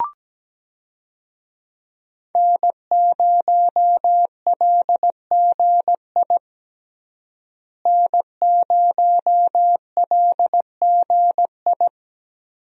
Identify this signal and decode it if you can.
Morse code, heard as a keyed tone, about 17 wpm, 700 Hz. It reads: N0LGI N0LGI
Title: USA Call Signs - 5 Characters - Repeated 2x - 17wpm